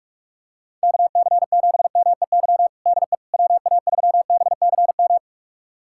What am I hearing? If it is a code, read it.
Morse code, heard as a keyed tone, about 41 wpm, 690 Hz. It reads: KC7MEY DE WA3BCM
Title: USA Callsign DE Callsign 45wpm